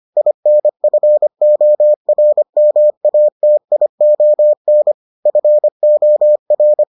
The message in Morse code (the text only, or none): INFORMATION FOR